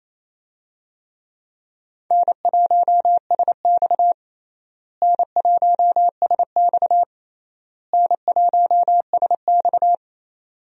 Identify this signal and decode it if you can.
Morse code, heard as a keyed tone, about 28 wpm, 700 Hz. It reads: N1SX N1SX N1SX
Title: USA Call Signs - 4 Characters - Repeated 3x - 28wpm